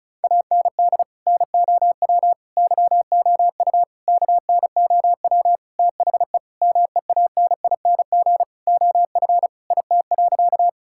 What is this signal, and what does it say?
Morse code, heard as a keyed tone, about 36 wpm, 700 Hz. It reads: AND NOW YOU KNOW THE MEANING OF IT.
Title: Sentences from Top 1000 Words - Review Part 6 - 35wpm